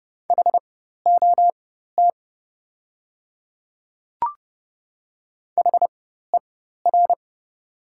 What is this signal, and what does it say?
Morse code, heard as a keyed tone, about 28 wpm, spaced as for 14 wpm, 700 Hz. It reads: HOT HER
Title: 4x Character Spacing - Top 100 Words - 30wpm